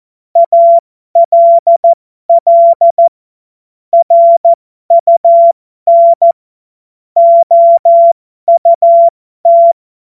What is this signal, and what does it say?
Morse code, heard as a keyed tone, about 14 wpm, spaced as for 12 wpm, 680 Hz. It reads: ALL RUN OUT
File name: Morsekurs / Witze